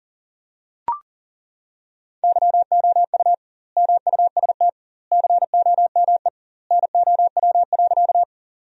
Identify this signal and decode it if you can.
Morse code, heard as a keyed tone, about 40 wpm, 700 Hz. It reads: YOU MUST COME NOW.
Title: 2x Word-Spacing - Sentences from Top 200 Words - 40wpm